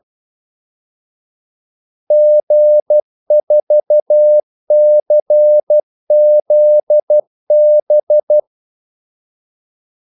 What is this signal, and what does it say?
Morse code, heard as a keyed tone, about 12 wpm, 600 Hz. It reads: G4CZB
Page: Kaldesignaler 12 wpm